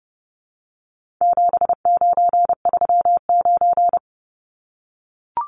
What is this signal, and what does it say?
Morse code, heard as a keyed tone, about 30 wpm, 700 Hz. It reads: 7939
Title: Numbers - 4-Digit 30wpm